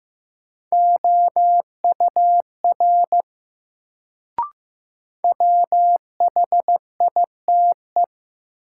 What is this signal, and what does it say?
Morse code, heard as a keyed tone, about 15 wpm, 700 Hz. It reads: OUR WHITE
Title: Sets of 2 Words - New Word Drill - 15wpm